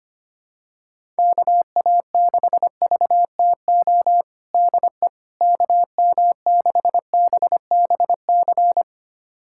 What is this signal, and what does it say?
Morse code, heard as a keyed tone, about 25 wpm, 700 Hz. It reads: KA6VTO DE KM6BBC
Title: USA Callsign DE Callsign 25wpm